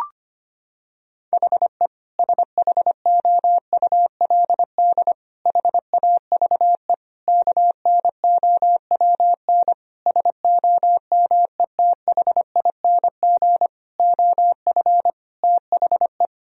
Morse code, heard as a keyed tone, about 25 wpm, 700 Hz. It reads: HE SHOULD HAVE KNOWN SOMETHING OF THE
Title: Sentences from Top 1000 Words - Review Part 3 - 25wpm